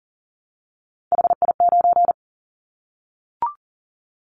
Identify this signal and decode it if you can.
Morse code, heard as a keyed tone, about 40 wpm, 700 Hz. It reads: HI9
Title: Callsign Prefixes for Top 100 DXCC - 40wpm